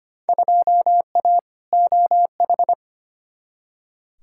Morse code, heard as a keyed tone, about 25 wpm, 700 Hz. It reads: 2A OH
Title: Field Day Exchanges - 25wpm